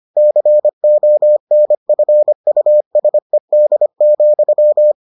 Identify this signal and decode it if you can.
Morse code, heard as a keyed tone, about 25 wpm, 600 Hz. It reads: CONFUSED,